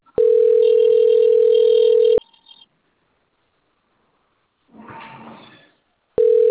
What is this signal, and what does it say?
Ringback tone (North America), 440 + 480 Hz.
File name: out- - - - - .16063